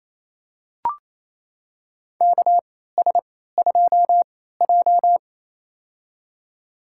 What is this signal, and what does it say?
Morse code, heard as a keyed tone, about 28 wpm, spaced as for 16 wpm, 700 Hz. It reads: KS2J
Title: USA Call Signs - 4 Characters - 3x Character Spacing - 28wpm